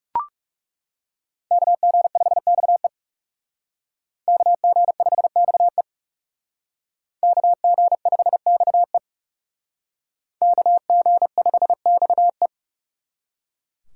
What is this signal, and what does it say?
Morse code, heard as a keyed tone, about 38 wpm, 690 Hz. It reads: KG5XE KG5XE KG5XE KG5XE